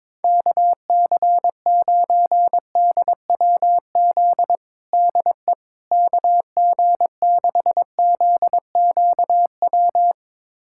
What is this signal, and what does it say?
Morse code, heard as a keyed tone, about 22 wpm, 700 Hz. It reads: KC9DWZ DE KG6ZQW
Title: USA Callsign DE Callsign - 6 Characters - 22wpm